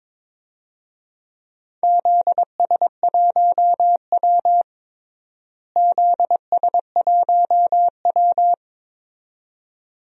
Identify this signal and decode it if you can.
Morse code, heard as a keyed tone, about 22 wpm, 700 Hz. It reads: ZS1W ZS1W
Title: DXCC-100 Call Signs - 4 Characters - Repeated 2x - 22wpm